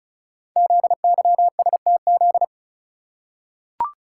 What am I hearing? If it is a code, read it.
Morse code, heard as a keyed tone, about 36 wpm, 700 Hz. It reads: ZYSTZ
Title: Code Groups - 5 Letters - 35wpm